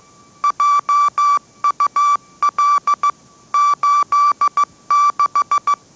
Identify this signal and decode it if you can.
Morse code, heard as a keyed tone, about 15 wpm, 1160 Hz. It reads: JUL 86
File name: mrx / mrx066 16wpm